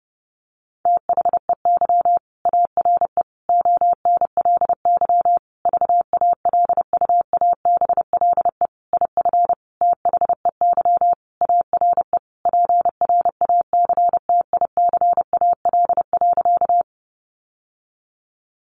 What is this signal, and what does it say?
Morse code, heard as a keyed tone, about 30 wpm, 700 Hz. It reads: THEY ARE ONLY VALUABLE IF THEY ARE PRACTICAL.
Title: Sentences from Top 1400 Words 30wpm